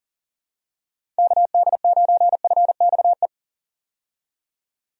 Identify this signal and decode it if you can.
Morse code, heard as a keyed tone, about 40 wpm, 700 Hz. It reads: KD9FXE
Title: USA Call Signs - 6 Characters - 40wpm